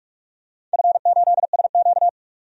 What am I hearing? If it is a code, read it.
Morse code, heard as a keyed tone, about 46 wpm, 700 Hz. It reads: F8SQ
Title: DXCC-100 Call Signs - 4 Characters - 45wpm